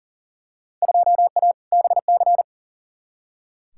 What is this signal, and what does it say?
Morse code, heard as a keyed tone, about 40 wpm, 700 Hz. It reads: 2A BC
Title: Field Day Exchanges - 40wpm